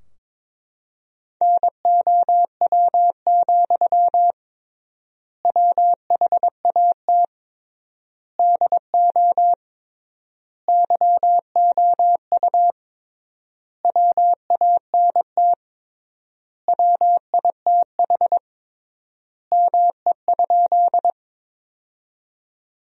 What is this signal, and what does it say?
Morse code, heard as a keyed tone, about 22 wpm, 700 Hz. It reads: NOW, WHAT DO YOU WANT WITH ME?
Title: 3x Word-Spacing - Sentences from Top 200 Words - 22wpm